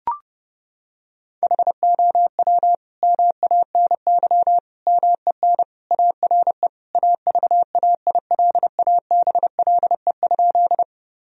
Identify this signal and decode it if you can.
Morse code, heard as a keyed tone, about 30 wpm, 700 Hz. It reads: HOW MANY MEN ARE AVAILABLE?
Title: Sentences from Top 1000 Words - Review Part 3 - 30wpm